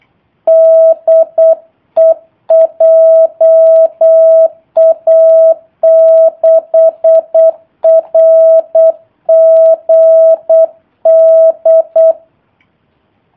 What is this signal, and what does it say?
Morse code, heard as a keyed tone, about 8 wpm, 650 Hz. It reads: DE JA6RGD
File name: jh8cbh / 2019cw / sp03- w